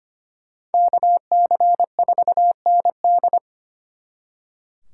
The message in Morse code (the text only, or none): KC4ND